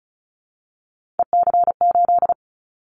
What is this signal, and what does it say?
Morse code, heard as a keyed tone, about 34 wpm, 700 Hz. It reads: EC8
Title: Callsign Prefixes for Top 50 DXCC - 35wpm